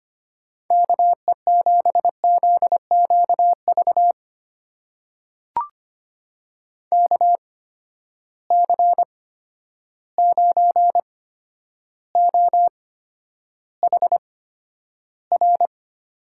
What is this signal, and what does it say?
Morse code, heard as a keyed tone, about 25 wpm, spaced as for 11 wpm, 700 Hz. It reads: KE7ZQV KC9OHR
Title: USA Call Signs - 6 Characters - 8x Character Spacing - 25wpm